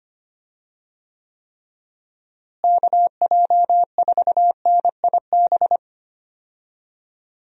Morse code, heard as a keyed tone, about 25 wpm, 700 Hz. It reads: KJ4NIB
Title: USA Call Signs - 6 Characters - Repeated 3x - 25wpm